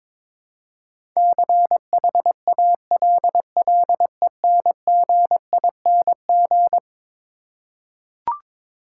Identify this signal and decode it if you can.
Morse code, heard as a keyed tone, about 21 wpm, 700 Hz. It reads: CHALLENGING
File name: practice / 22wpm / Eleven Letter Words - 22wpm